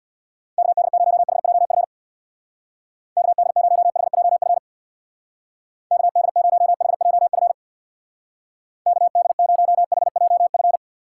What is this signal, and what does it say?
Morse code, heard as a keyed tone, about 60 wpm, 700 Hz. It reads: KD0HJF KD0HJF KD0HJF KD0HJF
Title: Speed Racing - USA 2x3 Call Signs 50wpm